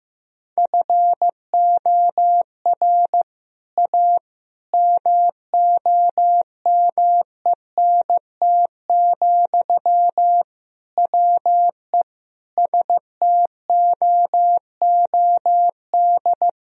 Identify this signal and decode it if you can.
Morse code, heard as a keyed tone, about 15 wpm, 700 Hz. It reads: FOR A MOMENT, WE STOOD
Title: Sentences from Top 1000 Words - Review Part 2 - 15wpm